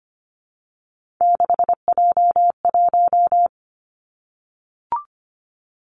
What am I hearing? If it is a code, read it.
Morse code, heard as a keyed tone, about 25 wpm, 700 Hz. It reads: 6J1
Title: Three Challenging Letter-Number Combinations 25wpm